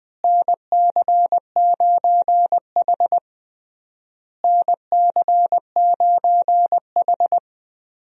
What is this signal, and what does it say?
Morse code, heard as a keyed tone, about 20 wpm, 700 Hz. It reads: NC9H NC9H
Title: USA Call Signs - 4 Characters - Repeated 3x - 20wpm